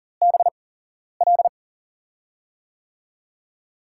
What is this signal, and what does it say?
Morse code, heard as a keyed tone, about 40 wpm, spaced as for 11 wpm, 700 Hz. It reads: BL
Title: USA Call Signs - 5 Characters - 8x Character Spacing - 40wpm